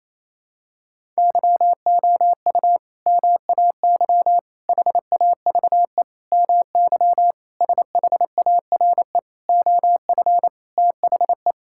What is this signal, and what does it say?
Morse code, heard as a keyed tone, about 28 wpm, 700 Hz. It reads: YOU MAY HAVE MY SHARE OF THE
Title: Sentences from Top 1000 Words - Review Part 6 - 28wpm